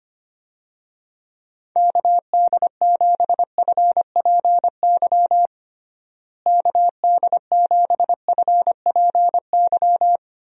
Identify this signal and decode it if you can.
Morse code, heard as a keyed tone, about 25 wpm, 700 Hz. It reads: KD7FPY KD7FPY
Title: USA Call Signs - 6 Characters - Repeated 3x - 25wpm